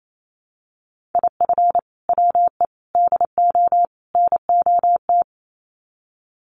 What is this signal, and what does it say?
Morse code, heard as a keyed tone, about 28 wpm, 700 Hz. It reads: IF WE DO NOT
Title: Sets of 4 Words 28wpm